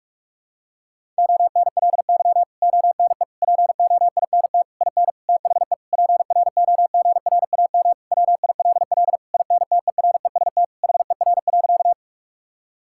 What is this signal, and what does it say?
Morse code, heard as a keyed tone, about 46 wpm, 700 Hz. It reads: ONLY ONE POINT IN THE PROGRAM WILL INTEREST HER.
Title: Sentences from Top 1000 Words - Review Part 5 - 45wpm